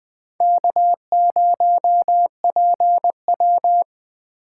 Morse code, heard as a keyed tone, about 21 wpm, 700 Hz. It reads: K0PW